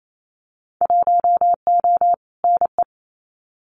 Morse code, heard as a keyed tone, about 28 wpm, 700 Hz. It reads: 1O NE
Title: Winter Field Day Exchanges 28wpm